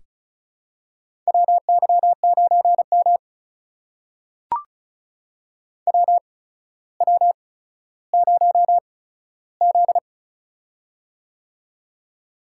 Morse code, heard as a keyed tone, about 36 wpm, spaced as for 15 wpm, 700 Hz. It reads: WY9M WW0Z
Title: USA Call Signs - 4 Characters - 8x Character Spacing - 35wpm